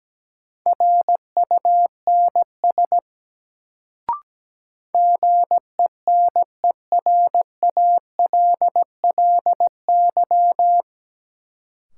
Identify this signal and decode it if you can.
Morse code, heard as a keyed tone, about 17 wpm, 700 Hz. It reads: RUNS GENERALLY
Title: Sentences from Top 600 Words - New Word Drill - 17wpm